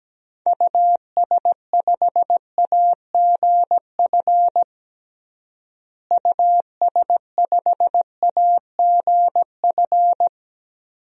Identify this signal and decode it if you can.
Morse code, heard as a keyed tone, about 17 wpm, 700 Hz. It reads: US5AGF US5AGF
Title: DXCC-100 Call Signs - 6 Characters - Repeated 2x - 17wpm